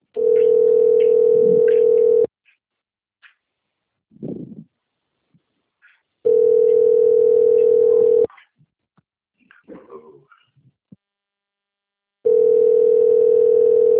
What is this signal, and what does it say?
Ringback tone (North America), 440 + 480 Hz.